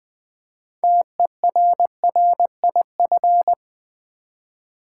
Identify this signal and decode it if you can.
Morse code, heard as a keyed tone, about 20 wpm, 700 Hz. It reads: TERRIF
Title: Build A Word - 3 Letters At A Time - 8-Letter Words - 20wpm